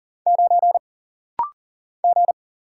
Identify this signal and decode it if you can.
Morse code, heard as a keyed tone, about 40 wpm, 700 Hz. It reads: 9 G
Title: Single Letter-Number - Rapid-Fire 40wpm